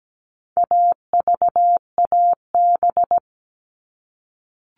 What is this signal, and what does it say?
Morse code, heard as a keyed tone, about 17 wpm, 700 Hz. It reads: AVAB
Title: All Cut Numbers - 4-Digit 17wpm